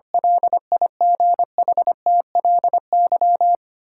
Morse code, heard as a keyed tone, about 25 wpm, 700 Hz. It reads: LIGHTLY
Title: Sentences from Top 1300 Words - New Word Drill - 25wpm